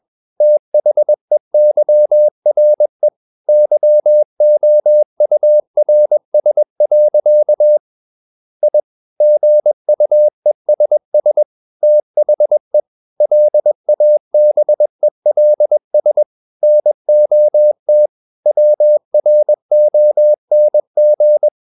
Morse code, heard as a keyed tone, about 21 wpm, 600 Hz. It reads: THEYRE YOURS. I GUESS THE LABELS NOT WRONG
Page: CW med Gnister